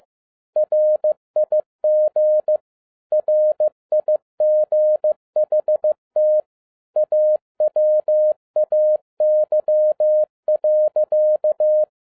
Morse code, heard as a keyed tone, about 15 wpm, 610 Hz. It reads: RIG RIGHT AWAY.
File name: 2011-10-mp3 / n07-A DOT 15wpm.mp3